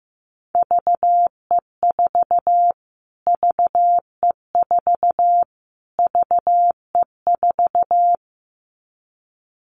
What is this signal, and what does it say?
Morse code, heard as a keyed tone, about 15 wpm, 700 Hz. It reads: VE4 VE4 VE4
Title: Callsign Prefixes for Top 25 DXCC - Repeated 3x - 15wpm